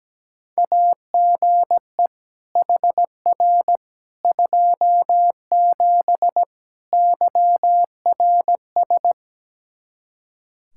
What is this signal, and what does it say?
Morse code, heard as a keyed tone, about 17 wpm, 700 Hz. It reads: AGE HR 27 YRS
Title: Practice QSO - Exchange 3 - More Info - 17wpm